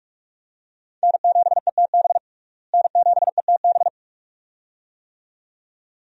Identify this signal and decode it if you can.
Morse code, heard as a keyed tone, about 41 wpm, 700 Hz. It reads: N7ETB N7ETB
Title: USA Call Signs - 5 Characters - Repeated 2x - 45wpm